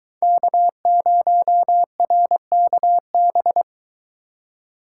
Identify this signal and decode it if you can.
Morse code, heard as a keyed tone, about 22 wpm, 700 Hz. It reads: K0RKB